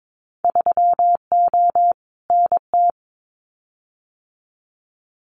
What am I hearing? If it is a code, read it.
Morse code, heard as a keyed tone, about 22 wpm, 700 Hz. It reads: 3O NT
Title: Winter Field Day Exchanges Encore 22wpm